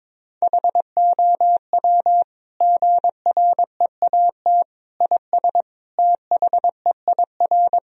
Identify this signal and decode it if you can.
Morse code, heard as a keyed tone, about 22 wpm, 700 Hz. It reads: HOW GREAT IS THEIR
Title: Sentences from Top 1000 Words - Review Part 4 - 22wpm